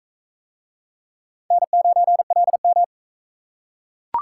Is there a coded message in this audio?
Yes: Morse code from 1.5 to 2.8 s, heard as a keyed tone, about 41 wpm, 700 Hz. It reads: N9LM